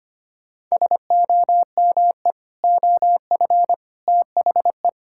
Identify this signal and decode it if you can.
Morse code, heard as a keyed tone, about 25 wpm, 700 Hz. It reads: SOME OF THE
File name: practice / 25wpm / Sentences with 3-Word Phrases - 25wpm